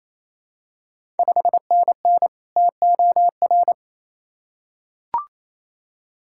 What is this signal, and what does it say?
Morse code, heard as a keyed tone, about 28 wpm, 700 Hz. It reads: 5NN TOR
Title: Ontario QSO Party Exchange 28wpm